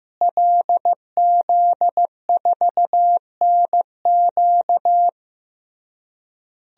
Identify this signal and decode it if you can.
Morse code, heard as a keyed tone, about 16 wpm, 700 Hz. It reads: LZ4NQ
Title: DXCC-100 Call Signs - 5 Characters - Repeated 3x - 15wpm